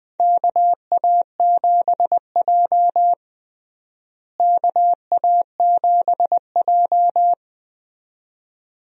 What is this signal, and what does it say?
Morse code, heard as a keyed tone, about 20 wpm, 700 Hz. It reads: KA7J KA7J
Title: USA Call Signs - 4 Characters - Repeated 2x - 20wpm